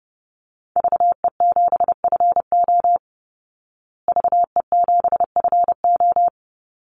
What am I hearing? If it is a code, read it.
Morse code, heard as a keyed tone, about 30 wpm, 700 Hz. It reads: VE7FO VE7FO
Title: Call Signs Repeated 3x - 30wpm